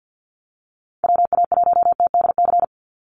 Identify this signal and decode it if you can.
Morse code, heard as a keyed tone, about 48 wpm, 700 Hz. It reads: WA1TDC
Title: USA Call Signs 50wpm